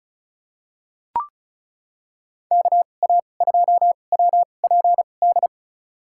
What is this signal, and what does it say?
Morse code, heard as a keyed tone, about 34 wpm, spaced as for 25 wpm, 700 Hz. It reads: KA2WPD
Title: USA Call Signs - 6 Characters - 2x Character Spacing - 35wpm